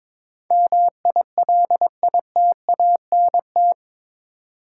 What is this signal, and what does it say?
Morse code, heard as a keyed tone, about 22 wpm, 700 Hz. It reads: MILITANT
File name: practice / 22wpm / Eight Letter Words - 22wpm